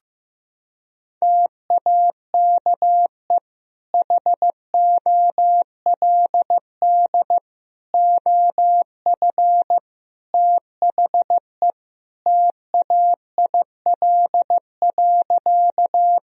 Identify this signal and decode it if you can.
Morse code, heard as a keyed tone, about 15 wpm, 700 Hz. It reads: TAKE HOLD OF THE TAIL.
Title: Sentences from Top 1000 Words - Review Part 3 - 15wpm